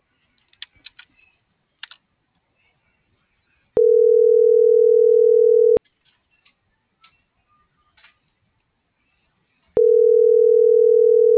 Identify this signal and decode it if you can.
Ringback tone (North America), 440 + 480 Hz.